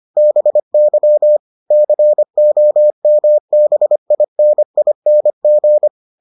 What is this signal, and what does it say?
Morse code, heard as a keyed tone, about 25 wpm, 600 Hz. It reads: BY COMBINING